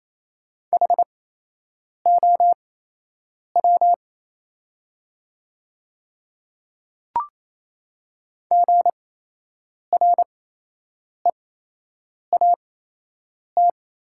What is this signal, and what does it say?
Morse code, heard as a keyed tone, about 27 wpm, spaced as for 8 wpm, 700 Hz. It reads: HOW GREAT
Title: 8x Character Spacing - Top 300 Words - 28wpm